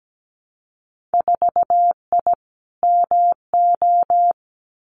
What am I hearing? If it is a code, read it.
Morse code, heard as a keyed tone, about 17 wpm, 700 Hz. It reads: 4I MO
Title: Winter Field Day Exchanges 17wpm